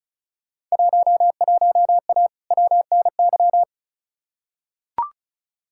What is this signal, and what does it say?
Morse code, heard as a keyed tone, about 36 wpm, 700 Hz. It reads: 11A WNY
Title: Field Day Exchanges Encore - 35wpm